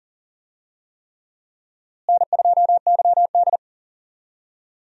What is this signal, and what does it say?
Morse code, heard as a keyed tone, about 40 wpm, 700 Hz. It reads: N2YD
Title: USA Call Signs - 4 Characters - Repeated 3x - 40wpm